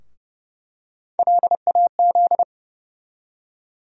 Morse code, heard as a keyed tone, about 30 wpm, 700 Hz. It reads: LAZ
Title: USA Call Sign Postfixes - 3 Letters 30wpm